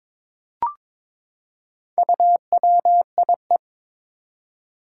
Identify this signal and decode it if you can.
Morse code, heard as a keyed tone, about 21 wpm, 700 Hz. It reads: UWIE
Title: Four Challenging Letter-Number Combinations 22wpm